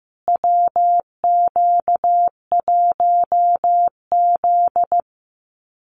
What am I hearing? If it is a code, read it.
Morse code, heard as a keyed tone, about 15 wpm, 700 Hz. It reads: WQ1Z
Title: USA Call Signs Repeated 2x 15wpm